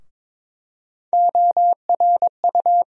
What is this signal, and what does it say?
Morse code, heard as a keyed tone, about 22 wpm, 700 Hz. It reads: ORU
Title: USA Call Sign Postfixes - 3 Letters 22wpm